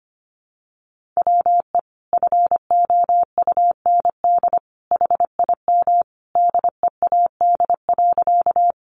Morse code, heard as a keyed tone, about 25 wpm, 700 Hz. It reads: WE FOUND HIM DEAD.
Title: Sentences from Top 1500 Words 25wpm